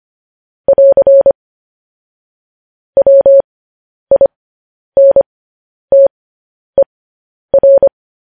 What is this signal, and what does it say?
Morse code, heard as a keyed tone, about 25 wpm, spaced as for 10 wpm, 550 Hz. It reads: <AR> WINTER
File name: podcast / News headlines-1x-25x10